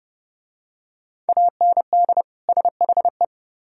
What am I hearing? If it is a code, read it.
Morse code, heard as a keyed tone, about 29 wpm, 700 Hz. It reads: AND SHE